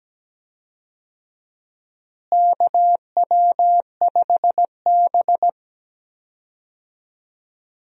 Morse code, heard as a keyed tone, about 17 wpm, 700 Hz. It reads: KW5B
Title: USA Call Signs - 4 Characters - Repeated 3x - 17wpm